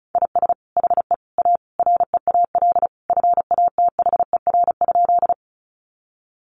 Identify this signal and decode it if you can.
Morse code, heard as a keyed tone, about 34 wpm, 700 Hz. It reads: IS HE A REAL FATHER?
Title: Sentences from Top 300 Words 35wpm